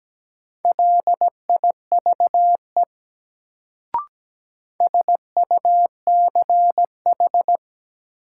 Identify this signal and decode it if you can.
Morse code, heard as a keyed tone, about 17 wpm, 700 Hz. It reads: LIVE SUCH
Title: Sentences from Top 200 Words - New Word Drill - 17wpm